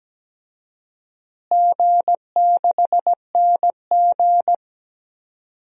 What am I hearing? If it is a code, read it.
Morse code, heard as a keyed tone, about 17 wpm, 700 Hz. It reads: G6NG
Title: DXCC-100 Call Signs - 4 Characters - Repeated 3x - 17wpm